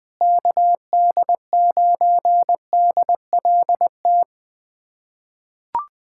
Morse code, heard as a keyed tone, about 20 wpm, 700 Hz. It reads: KD9DLT
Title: USA Callsign DE Callsign 20wpm